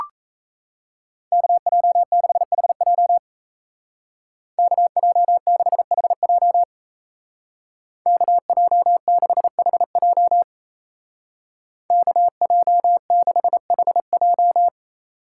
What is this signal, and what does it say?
Morse code, heard as a keyed tone, about 34 wpm, 700 Hz. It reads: KJ6HJ KJ6HJ KJ6HJ KJ6HJ
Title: Speed Racing - USA 2x2 Call Signs 28wpm